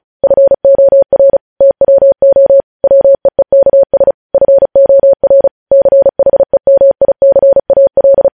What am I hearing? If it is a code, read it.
Morse code, heard as a keyed tone, about 36 wpm, 550 Hz. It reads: FOR TWO WEEKS FOR CHEMICAL
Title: News Headlines at 35